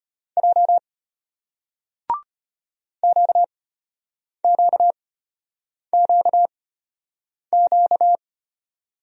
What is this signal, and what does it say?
Morse code, heard as a keyed tone, about 33 wpm, spaced as for 8 wpm, 700 Hz. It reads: J QQQQ